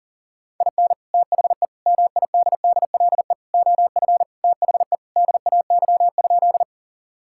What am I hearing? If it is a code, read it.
Morse code, heard as a keyed tone, about 40 wpm, 700 Hz. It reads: IN THE MIDDLE OF THE DAY?
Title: Sentences from Top 1000 Words - Review Part 6 - 40wpm